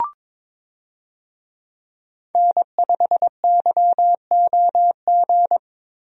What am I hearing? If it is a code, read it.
Morse code, heard as a keyed tone, about 22 wpm, 700 Hz. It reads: N5YOG
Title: USA Call Signs - 5 Characters - Repeated 2x - 22wpm